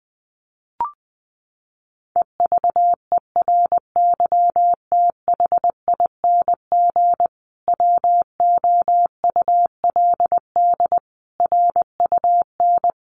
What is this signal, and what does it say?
Morse code, heard as a keyed tone, about 19 wpm, 700 Hz. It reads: EVERYTHING WOULD RUN
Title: Sentences from Top 1400 Words 20wpm